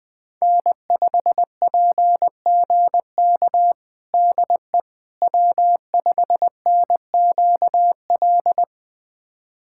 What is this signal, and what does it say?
Morse code, heard as a keyed tone, about 21 wpm, 700 Hz. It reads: N5PGK DE W5NQL
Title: USA Callsign DE Callsign - 5 Characters - 20wpm